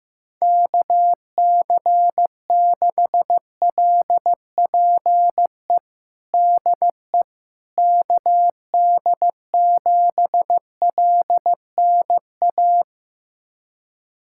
Morse code, heard as a keyed tone, about 15 wpm, 700 Hz. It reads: KC6LPE DE KD7LNA
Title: USA Callsign DE Callsign - 6 Characters - 15wpm